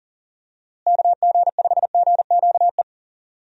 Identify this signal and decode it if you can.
Morse code, heard as a keyed tone, about 40 wpm, 700 Hz. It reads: KG5GQE
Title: USA Call Signs - 6 Characters - Repeated 3x - 40wpm